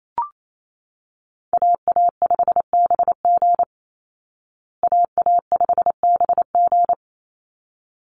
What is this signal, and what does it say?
Morse code, heard as a keyed tone, about 28 wpm, 690 Hz. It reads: AA5BG AA5BG
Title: Call Signs Repeated 3x - 28wpm